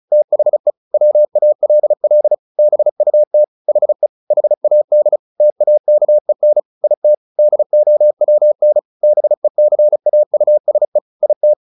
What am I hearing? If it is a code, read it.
Morse code, heard as a keyed tone, about 34 wpm, 600 Hz. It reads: THE WALL BUT HE HAD TAKEN IT DOWN BECAUSE IT
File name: the old man and the sea / the old man and the sea 1 35wpm